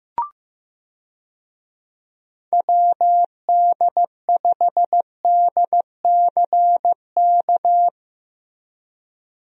Morse code, heard as a keyed tone, about 15 wpm, 700 Hz. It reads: WD5DCK
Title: USA Call Signs - 6 Characters - 15wpm